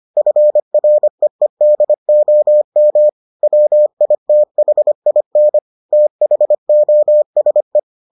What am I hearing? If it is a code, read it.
Morse code, heard as a keyed tone, about 25 wpm, 600 Hz. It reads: FREEDOM WITHIN THOSE